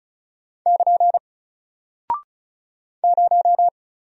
Morse code, heard as a keyed tone, about 36 wpm, 700 Hz. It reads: <KN> 0
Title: Single Character (Letter:Number:Punctuation:Prosign) Alt 35wpm